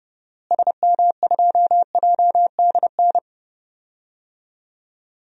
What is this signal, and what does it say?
Morse code, heard as a keyed tone, about 31 wpm, 700 Hz. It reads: SM2JDN
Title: DXCC-100 Call Signs - 6 Characters - Repeated 2x - 30wpm